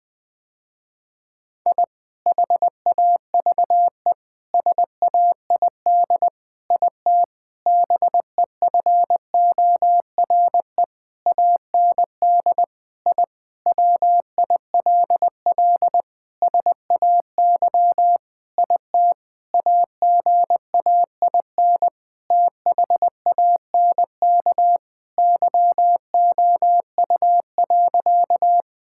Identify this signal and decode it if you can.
Morse code, heard as a keyed tone, about 20 wpm, 700 Hz. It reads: I HAVE SAID IT BEFORE AND I WILL SAY IT AGAIN THANK YOU.
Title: Sentences with 8-Word Phrases - Spoken Beginning - 20wpm